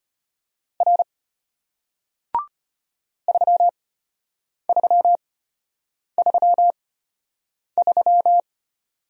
Morse code, heard as a keyed tone, about 33 wpm, spaced as for 8 wpm, 700 Hz. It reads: R 3333